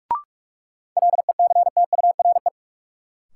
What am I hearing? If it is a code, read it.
Morse code, heard as a keyed tone, about 46 wpm, 700 Hz. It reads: LECTURE
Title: Sentences from Top 1600 Words - New Word Drill - 45wpm